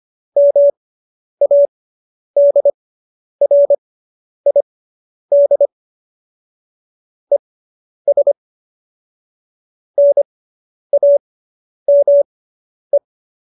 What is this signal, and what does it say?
Morse code, heard as a keyed tone, about 25 wpm, spaced as for 10 wpm, 570 Hz. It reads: MADRID ES NAME